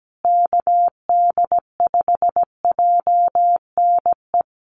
Morse code, heard as a keyed tone, about 17 wpm, 700 Hz. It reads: KD5JNE
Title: USA Call Signs Repeated 3x 17wpm